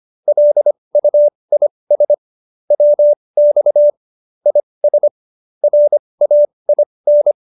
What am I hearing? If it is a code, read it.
Morse code, heard as a keyed tone, about 25 wpm, spaced as for 20 wpm, 590 Hz. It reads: LUIS WX IS RAIN